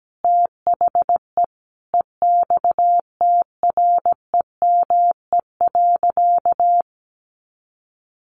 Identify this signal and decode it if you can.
Morse code, heard as a keyed tone, about 17 wpm, 700 Hz. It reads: THE EXTREME.
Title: Sentences from Top 1300 Words 17wpm